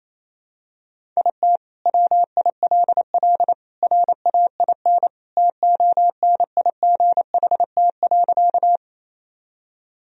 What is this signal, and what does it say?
Morse code, heard as a keyed tone, about 28 wpm, 700 Hz. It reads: IT WILL RAIN TONIGHT.
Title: Sentences from Top 1000 Words - Review Part 1 - 28wpm